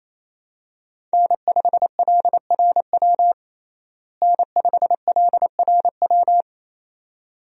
Morse code, heard as a keyed tone, about 28 wpm, 700 Hz. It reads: N5LRW N5LRW
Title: USA Call Signs - 5 Characters - Repeated 2x - 28wpm